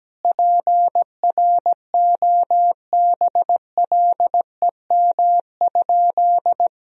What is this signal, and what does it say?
Morse code, heard as a keyed tone, about 17 wpm, 700 Hz. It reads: PROBLEM?
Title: Sentences from Top 1000 Words - Review Part 4 - 17wpm